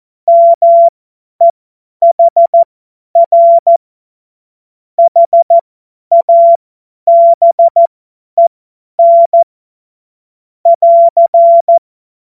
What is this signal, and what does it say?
Morse code, heard as a keyed tone, about 14 wpm, spaced as for 10 wpm, 680 Hz. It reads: MEHR HABEN <AR>
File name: Morsekurs / Witze / Witz 03. Januar 2023 50 F70